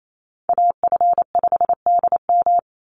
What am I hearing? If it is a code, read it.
Morse code, heard as a keyed tone, about 28 wpm, 700 Hz. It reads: AF5DM
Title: Call Signs - 28wpm